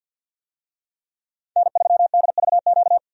Morse code, heard as a keyed tone, about 50 wpm, 700 Hz. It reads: N3DVQ
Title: USA Call Signs - 5 Characters - 50wpm